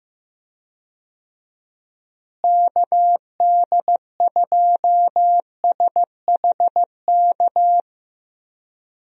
Morse code, heard as a keyed tone, about 15 wpm, 700 Hz. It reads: KD2SHK
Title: USA Call Signs - 6 Characters - Repeated 3x - 15wpm